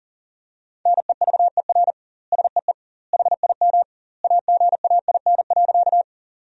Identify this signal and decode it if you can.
Morse code, heard as a keyed tone, about 40 wpm, 700 Hz. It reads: NEVER SEE HIM AGAIN.
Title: 2x Word-Spacing - Sentences from Top 300 Words - 40wpm